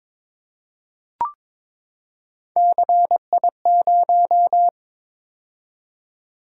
Morse code, heard as a keyed tone, about 22 wpm, 700 Hz. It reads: CI0
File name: practice / 22wpm / Three Letter-Number Combinations - 22wpm